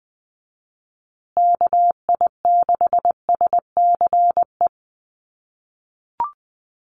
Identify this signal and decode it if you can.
Morse code, heard as a keyed tone, about 20 wpm, 700 Hz. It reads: KI6SCE
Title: USA Call Signs Repeated 2x 20wpm